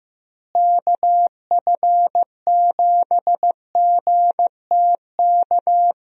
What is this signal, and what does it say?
Morse code, heard as a keyed tone, about 16 wpm, 700 Hz. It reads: KF7GTK
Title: USA Call Signs - 6 Characters - Repeated 2x - 15wpm